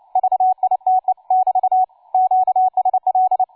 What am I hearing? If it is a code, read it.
Morse code, heard as a keyed tone, about 29 wpm, 750 Hz, with radio noise. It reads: VITE <BT> QSL
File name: cw forum / tu2 f5ldy 18mhz 16h46